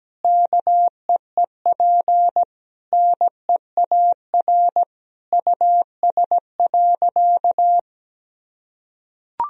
Morse code, heard as a keyed tone, about 17 wpm, 700 Hz. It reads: KEEP NEAR US.
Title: Sentences from Top 1000 Words - Review Part 2 - 17wpm